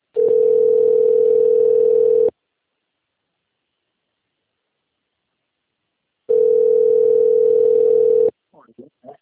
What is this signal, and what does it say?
Ringback tone (North America), 440 + 480 Hz, with speech.
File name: rg-1600-unknown- - - .73053